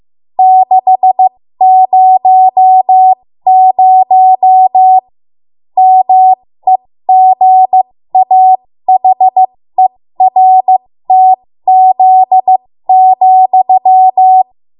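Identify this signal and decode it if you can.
Morse code, heard as a keyed tone, about 15 wpm, spaced as for 13 wpm, 750 Hz. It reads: 600 MEGAHERTZ,
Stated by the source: 13 wpm Morse code practice archive transmission.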